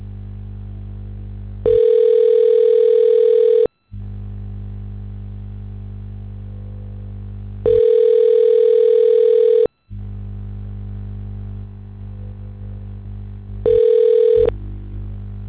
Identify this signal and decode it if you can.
Ringback tone (North America), 440 + 480 Hz.